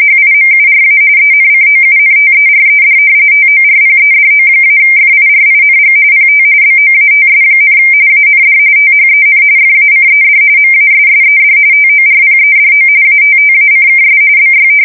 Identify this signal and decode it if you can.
RTTY, about 45.45 baud. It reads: HIMURA ENCOUNTERED TORPEDO-FIRING PT BOATS, AND THEN FACED MASSED TORPEDO ATTACKS BY
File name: fsk170 / battleshipnj